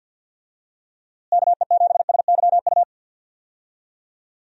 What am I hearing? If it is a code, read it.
Morse code, heard as a keyed tone, about 50 wpm, 700 Hz. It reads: KE7SYU
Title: USA Call Signs - 6 Characters - Repeated 2x - 50wpm